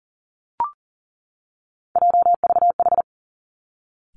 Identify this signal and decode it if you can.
Morse code, heard as a keyed tone, about 39 wpm, 700 Hz. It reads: JVH
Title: Three Challenging Letter-Number Combinations 40wpm